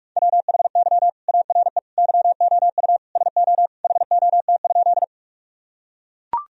Morse code, heard as a keyed tone, about 43 wpm, 700 Hz. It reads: WHY ARE YOU SO HOT?
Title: Sentences from Top 1000 Words - Review Part 3 - 45wpm